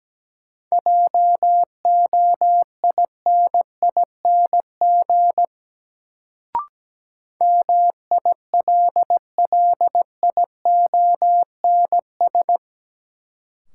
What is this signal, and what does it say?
Morse code, heard as a keyed tone, about 17 wpm, 700 Hz. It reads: JOINING MILLIONS
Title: Sets of 5 Words - New Word Drill - 17wpm